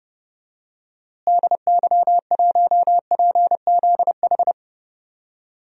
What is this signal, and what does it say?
Morse code, heard as a keyed tone, about 30 wpm, 700 Hz. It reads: DY1PZH
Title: DXCC-100 Call Signs - 6 Characters - Repeated 3x - 30wpm